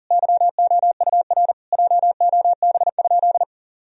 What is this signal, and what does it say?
Morse code, heard as a keyed tone, about 40 wpm, 700 Hz. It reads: YOUR JOB?
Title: Sentences from Top 1000 Words - Review Part 3 - 40wpm